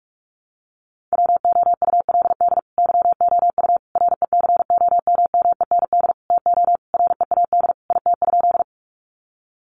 Morse code, heard as a keyed tone, about 44 wpm, 700 Hz. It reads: WOULD YOU RECOMMEND TO READ IT?
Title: Sentences from Top 1200 Words 45wpm